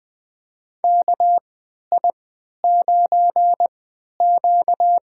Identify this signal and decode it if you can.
Morse code, heard as a keyed tone, about 20 wpm, spaced as for 11 wpm, 700 Hz. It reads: KI9Q
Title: USA Call Signs - 3x Character Spacing - 20wpm